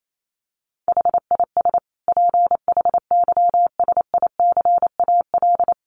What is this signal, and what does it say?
Morse code, heard as a keyed tone, about 28 wpm, 700 Hz. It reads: HIS PHYSICAL
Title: Sentences from Top 1400 Words 28wpm